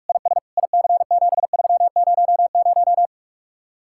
Morse code, heard as a keyed tone, about 49 wpm, 700 Hz. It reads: IS IC7300